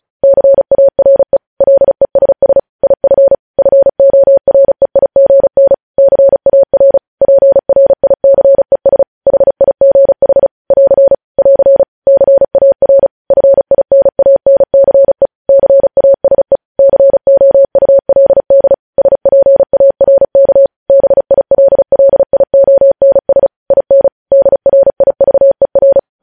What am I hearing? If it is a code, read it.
Morse code, heard as a keyed tone, about 36 wpm, 550 Hz. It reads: CARE LESS IF FOREIGN CAR PRICES HIGH <AR> <AR> CAR FINANCE CASE COULD SPARK BILLIONS IN DRIVER
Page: News headlines in Morse code at 35 wpm